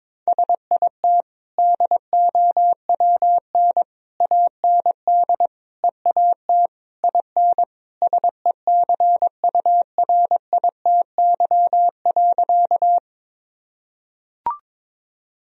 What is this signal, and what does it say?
Morse code, heard as a keyed tone, about 21 wpm, 700 Hz. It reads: SIT DOWN AND EAT IN SECURITY.
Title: Sentences from Top 1000 Words - Review Part 4 - 22wpm